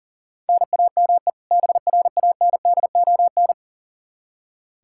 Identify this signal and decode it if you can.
Morse code, heard as a keyed tone, about 40 wpm, 700 Hz. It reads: NAME BRANDON
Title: K1USN SST QSO Practice 40wpm